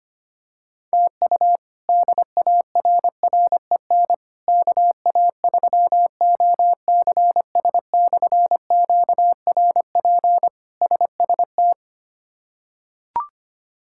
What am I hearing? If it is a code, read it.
Morse code, heard as a keyed tone, about 25 wpm, 700 Hz. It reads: TU DARREN KA3OCS/QRP SST
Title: K1USN SST QSO Practice - 25wpm